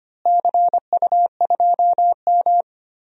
Morse code, heard as a keyed tone, about 25 wpm, 700 Hz. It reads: CU2M
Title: DXCC-100 Call Signs - 4 Characters - Repeated 2x - 25wpm